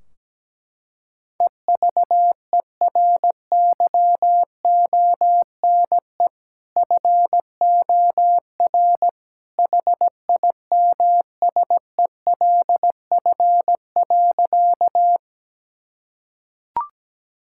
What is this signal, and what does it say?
Morse code, heard as a keyed tone, about 17 wpm, 700 Hz. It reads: EVERYONE FOR HIMSELF.
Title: Sentences from Top 1600 Words 17wpm